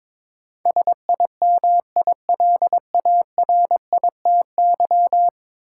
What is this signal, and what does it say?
Morse code, heard as a keyed tone, about 22 wpm, 700 Hz. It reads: SIMILARITY
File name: practice / 22wpm / 10 Letter Words - First 3 Letters Given - 22wpm